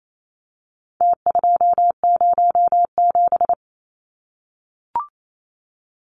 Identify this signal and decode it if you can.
Morse code, heard as a keyed tone, about 28 wpm, 700 Hz. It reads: T207
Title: Numbers - Contest Style - 4-Digit 28wpm